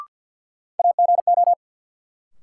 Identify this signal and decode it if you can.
Morse code, heard as a keyed tone, about 49 wpm, 700 Hz. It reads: AGO
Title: Sets of 2 Words - New Word Drill - 50wpm